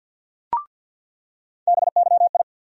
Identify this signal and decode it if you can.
Morse code, heard as a keyed tone, about 47 wpm, 700 Hz. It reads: BYI
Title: USA Call Sign Postfixes - 3 Letters 50wpm